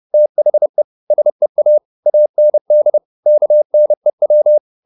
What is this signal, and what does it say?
Morse code, heard as a keyed tone, about 30 wpm, 600 Hz. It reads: THE SEA AND KNEW
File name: the old man and the sea / the old man and the sea 1 30wpm